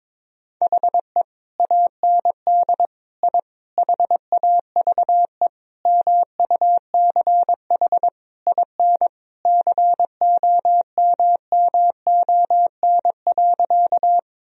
Morse code, heard as a keyed tone, about 22 wpm, 690 Hz. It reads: HE AND I HAVE MUCH IN COMMON.
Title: Sentences from Top 1000 Words - Review Part 3 - 22wpm